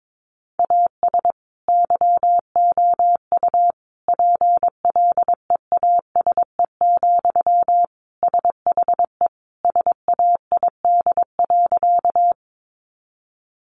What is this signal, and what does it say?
Morse code, heard as a keyed tone, about 22 wpm, 700 Hz. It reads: AS YOU PLEASE, SHE SAID.
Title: Sentences from Top 1500 Words 22wpm